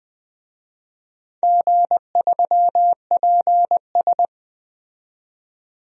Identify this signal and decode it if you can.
Morse code, heard as a keyed tone, about 20 wpm, 700 Hz. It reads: G3PS
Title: DXCC-100 Call Signs - 4 Characters - 20wpm